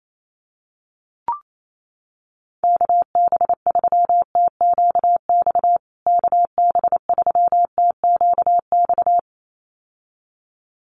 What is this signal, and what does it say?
Morse code, heard as a keyed tone, about 28 wpm, 700 Hz. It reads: KB3TQX KB3TQX
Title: USA Call Signs Repeated 2x 28wpm